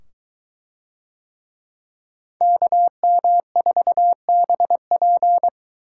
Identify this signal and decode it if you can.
Morse code, heard as a keyed tone, about 22 wpm, 700 Hz. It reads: KM4BP